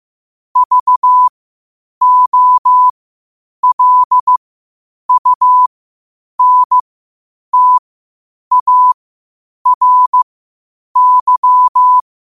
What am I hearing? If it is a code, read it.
Morse code, heard as a keyed tone, about 15 wpm, spaced as for 8 wpm, 1000 Hz. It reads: VOLUNTARY
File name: qotdfiles / 8wpm